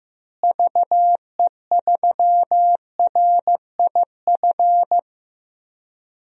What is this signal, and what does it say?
Morse code, heard as a keyed tone, about 16 wpm, 700 Hz. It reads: VE3RIF
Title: DXCC-100 Call Signs - 6 Characters - Repeated 3x - 15wpm